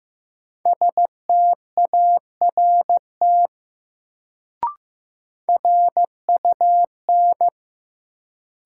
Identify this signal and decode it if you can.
Morse code, heard as a keyed tone, about 15 wpm, 700 Hz. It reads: START RUN
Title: Sentences from Top 300 Words - New Word Drill - 15wpm